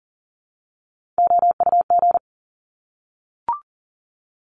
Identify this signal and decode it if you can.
Morse code, heard as a keyed tone, about 38 wpm, 690 Hz. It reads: OUG
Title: Top Three Letter Combinations in Words 40wpm